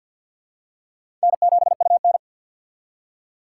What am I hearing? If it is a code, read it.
Morse code, heard as a keyed tone, about 52 wpm, 700 Hz. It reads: N7UN
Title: USA Call Signs - 4 Characters - 2x Character Spacing - 50wpm